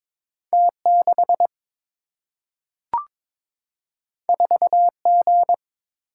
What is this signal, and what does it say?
Morse code, heard as a keyed tone, about 22 wpm, 700 Hz. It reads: T 6 4 G
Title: Single-Letter Single-Number 22wpm